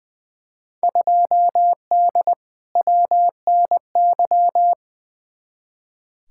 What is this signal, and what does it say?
Morse code, heard as a keyed tone, about 20 wpm, 700 Hz. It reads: 2D WNY
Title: Field Day Exchanges - 20wpm